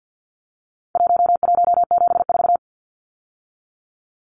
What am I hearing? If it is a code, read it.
Morse code, heard as a keyed tone, about 50 wpm, 700 Hz. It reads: 1174
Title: Numbers - Contest Style - 4-Digit 50wpm